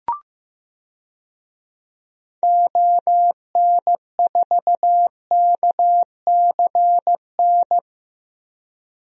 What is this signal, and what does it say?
Morse code, heard as a keyed tone, about 15 wpm, 700 Hz. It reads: ON4KCN
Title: DXCC-100 Call Signs - 6 Characters - 15wpm